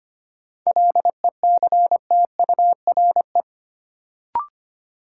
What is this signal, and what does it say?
Morse code, heard as a keyed tone, about 25 wpm, 700 Hz. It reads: LECTURE
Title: Sentences from Top 1600 Words - New Word Drill - 25wpm